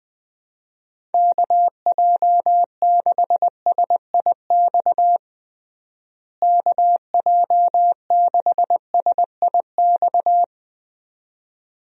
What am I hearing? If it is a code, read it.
Morse code, heard as a keyed tone, about 20 wpm, 700 Hz. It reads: KJ6SIX KJ6SIX
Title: USA Call Signs - 6 Characters - Repeated 2x - 20wpm